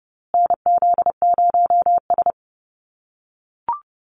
Morse code, heard as a keyed tone, about 29 wpm, 700 Hz. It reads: NZ0S
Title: USA Call Signs Repeated 3x 30wpm